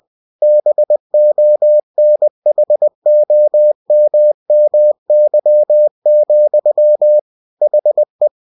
Morse code, heard as a keyed tone, about 20 wpm, 600 Hz. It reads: BONHOMMY, HE
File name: cw-books / winnie-the-pooh / winnie-20wpm-page-0045